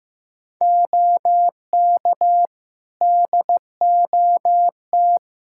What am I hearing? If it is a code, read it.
Morse code, heard as a keyed tone, about 15 wpm, 700 Hz. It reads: OK DOT
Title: Practice QSO - Exchange 2 - Info - 15wpm